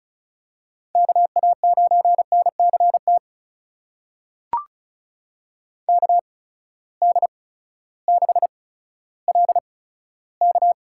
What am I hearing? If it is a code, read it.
Morse code, heard as a keyed tone, about 36 wpm, spaced as for 16 wpm, 700 Hz. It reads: KA9NCT KD6LK
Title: USA Call Signs - 8x Character Spacing - 35wpm